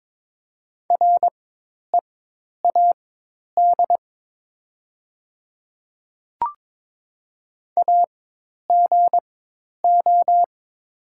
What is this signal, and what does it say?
Morse code, heard as a keyed tone, about 22 wpm, spaced as for 10 wpm, 700 Hz. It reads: READ AGO
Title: 4x Character Spacing - Top 500 Words - 22wpm